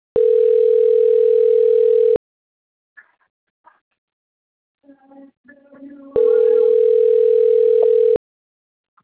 Ringback tone (North America), 440 + 480 Hz.